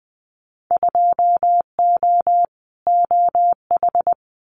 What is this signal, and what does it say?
Morse code, heard as a keyed tone, about 20 wpm, 700 Hz. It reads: 2O OH
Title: Winter Field Day Exchanges 20wpm